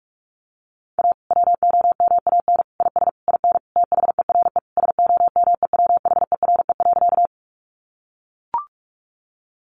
Morse code, heard as a keyed tone, about 44 wpm, 700 Hz. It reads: A WOMAN IS IN THERE SOMEWHERE.
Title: Sentences from Top 1400 Words 45wpm